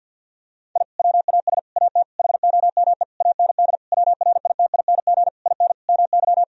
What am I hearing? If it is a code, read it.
Morse code, heard as a keyed tone, about 52 wpm, 700 Hz. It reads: I WAS AT HOME AND WRITING IN MY
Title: Sentences from Top 1000 Words - Review Part 1 - 50wpm